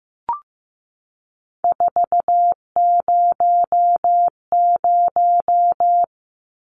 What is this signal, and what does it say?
Morse code, heard as a keyed tone, about 15 wpm, 700 Hz. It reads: 400
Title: Numbers - 3-Digit 15wpm